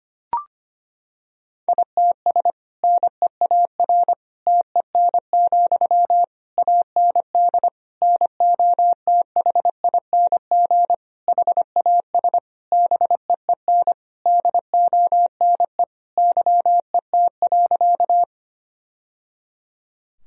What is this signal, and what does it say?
Morse code, heard as a keyed tone, about 25 wpm, 700 Hz. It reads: ITS NEAR TEN, AND NOTHING HAS BEEN DONE YET.
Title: Sentences from Top 1000 Words - Review Part 2 - 25wpm